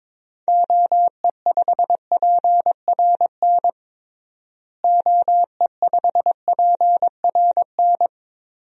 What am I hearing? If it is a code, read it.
Morse code, heard as a keyed tone, about 22 wpm, 700 Hz. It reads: OE5PRN OE5PRN
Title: DXCC-100 Call Signs - 6 Characters - Repeated 3x - 22wpm